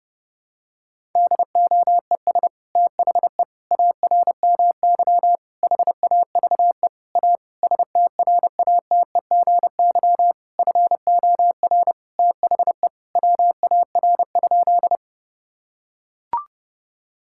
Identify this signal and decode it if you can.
Morse code, heard as a keyed tone, about 30 wpm, 700 Hz. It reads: DOES THE ARMY HAVE A STRATEGY FOR THE WAR?
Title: Sentences from Top 1000 Words - Review Part 5 - 30wpm